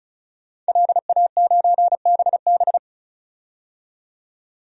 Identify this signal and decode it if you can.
Morse code, heard as a keyed tone, about 36 wpm, 700 Hz. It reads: LA9BB
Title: DXCC-100 Call Signs - 5 Characters - 35wpm